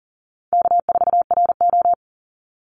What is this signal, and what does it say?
Morse code, heard as a keyed tone, about 40 wpm, 700 Hz. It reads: K4RO
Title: Easy Call Signs Repeated 2x - 40wpm